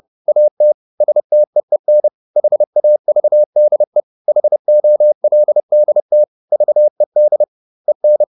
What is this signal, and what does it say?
Morse code, heard as a keyed tone, about 30 wpm, 600 Hz. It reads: AT STEEN HAVDE HOLDT VED EN
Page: Ferie 30 wpm